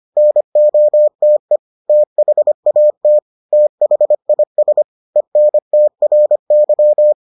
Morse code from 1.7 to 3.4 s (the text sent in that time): THAT